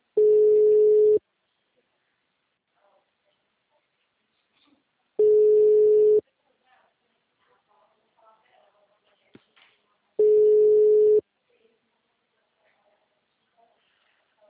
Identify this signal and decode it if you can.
Ringback tone (Europe, 425 Hz), with speech.